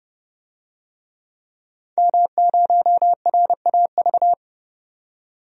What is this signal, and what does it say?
Morse code, heard as a keyed tone, about 32 wpm, 700 Hz. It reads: M0RAV
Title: DXCC-100 Call Signs - 5 Characters - 30wpm